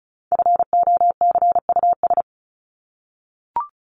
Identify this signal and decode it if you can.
Morse code, heard as a keyed tone, about 34 wpm, 700 Hz. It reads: FOCUS
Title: Top 501-750 Words 35wpm